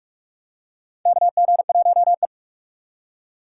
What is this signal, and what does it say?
Morse code, heard as a keyed tone, about 45 wpm, 700 Hz. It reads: KG1E
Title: USA Call Signs - 4 Characters - 3x Character Spacing - 45wpm